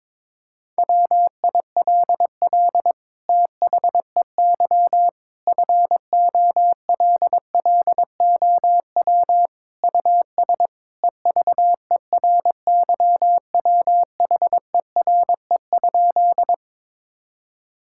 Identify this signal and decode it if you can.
Morse code, heard as a keyed tone, about 22 wpm, 700 Hz. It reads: WILL THEY FOLLOW US EVERYWHERE?
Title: Sentences from Top 1600 Words 22wpm